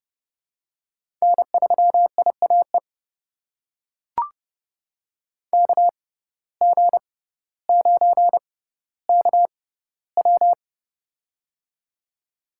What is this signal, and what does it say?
Morse code, heard as a keyed tone, about 32 wpm, spaced as for 16 wpm, 700 Hz. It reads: N3IAE KG9KW
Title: USA Call Signs - 5 Characters - 6x Character Spacing - 30wpm